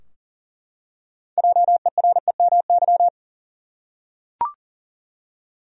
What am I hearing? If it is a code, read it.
Morse code, heard as a keyed tone, about 39 wpm, 700 Hz. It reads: JEREMY
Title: USA Common Names - 40wpm